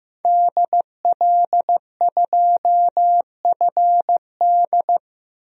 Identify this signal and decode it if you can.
Morse code, heard as a keyed tone, about 15 wpm, 700 Hz. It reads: DL2FD
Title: DXCC-100 Call Signs - 5 Characters - 15wpm